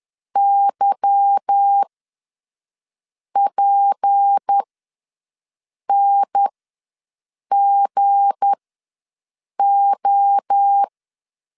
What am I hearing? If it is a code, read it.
Morse code, heard as a keyed tone, about 10 wpm, spaced as for 5 wpm, 800 Hz. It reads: YPNGO
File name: LEZIONI / 8 YJ45 / 8 YJ4 ws 10wpm 4020